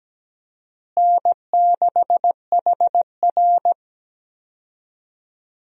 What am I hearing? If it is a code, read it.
Morse code, heard as a keyed tone, about 17 wpm, 700 Hz. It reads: N6HR
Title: USA Call Signs - 4 Characters - Repeated 2x - 17wpm